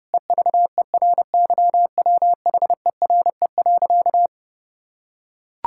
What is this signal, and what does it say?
Morse code, heard as a keyed tone, about 32 wpm, 700 Hz. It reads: EVERYWHERE.
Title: Sentences from Top 1600 Words 30wpm